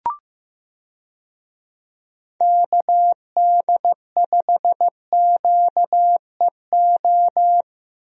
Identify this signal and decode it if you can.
Morse code, heard as a keyed tone, about 16 wpm, 700 Hz. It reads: KD5QEO
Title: USA Call Signs - 6 Characters - Repeated 3x - 15wpm